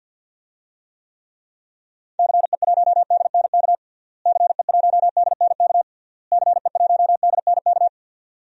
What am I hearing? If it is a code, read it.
Morse code, heard as a keyed tone, about 48 wpm, 700 Hz. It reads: CE1DNK CE1DNK CE1DNK
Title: DXCC-100 Call Signs - 6 Characters - Repeated 3x - 50wpm